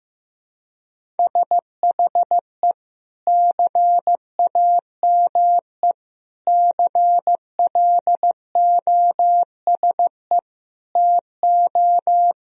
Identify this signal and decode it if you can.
Morse code, heard as a keyed tone, about 15 wpm, 700 Hz. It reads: SHE CAME CLOSE TO
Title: Sentences from Top 1000 Words - Review Part 1 - 15wpm